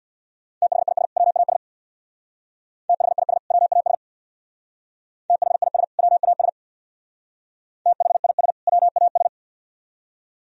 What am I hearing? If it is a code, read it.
Morse code, heard as a keyed tone, about 62 wpm, 700 Hz. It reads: THIS WAS THIS WAS THIS WAS THIS WAS
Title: Speed-Racing - Sets of 2 Words 50wpm